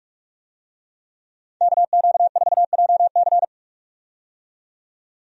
Morse code, heard as a keyed tone, about 46 wpm, 700 Hz. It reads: KQ4JC
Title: USA Call Signs - 5 Characters - 45wpm